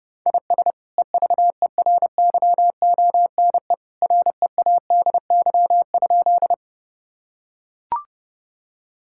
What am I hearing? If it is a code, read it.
Morse code, heard as a keyed tone, about 32 wpm, 700 Hz. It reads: IS EVERYONE READY?
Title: Sentences from Top 1000 Words - Review Part 5 - 30wpm